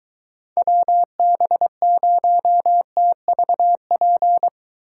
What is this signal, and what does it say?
Morse code, heard as a keyed tone, about 24 wpm, 700 Hz. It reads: WB0TVP